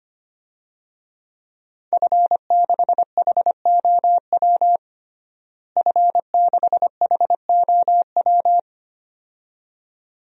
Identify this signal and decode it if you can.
Morse code, heard as a keyed tone, about 25 wpm, 700 Hz. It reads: F6HOW F6HOW
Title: DXCC-100 Call Signs - 5 Characters - Repeated 2x - 25wpm